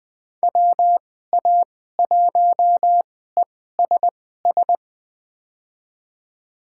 Morse code, heard as a keyed tone, about 20 wpm, spaced as for 14 wpm, 700 Hz. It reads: WA1ESS
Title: USA Call Signs - 6 Characters - 2x Character Spacing - 20wpm